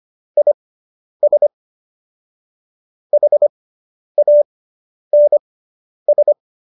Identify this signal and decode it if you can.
Morse code, heard as a keyed tone, about 25 wpm, spaced as for 10 wpm, 590 Hz. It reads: IS HANS